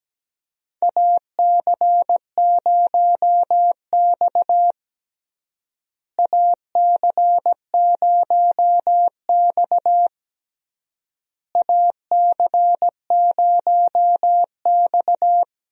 Morse code, heard as a keyed tone, about 17 wpm, 700 Hz. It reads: AC0X AC0X AC0X
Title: USA Call Signs - 4 Characters - Repeated 3x - 17wpm